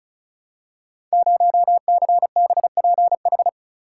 Morse code, heard as a keyed tone, about 36 wpm, 690 Hz. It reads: 0CBPH
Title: Code Groups - 5 Letter:Numbers - 35wpm